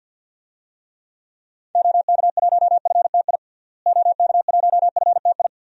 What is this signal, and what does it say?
Morse code, heard as a keyed tone, about 48 wpm, 700 Hz. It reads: OK1FTI OK1FTI
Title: DXCC-100 Call Signs - 6 Characters - Repeated 2x - 50wpm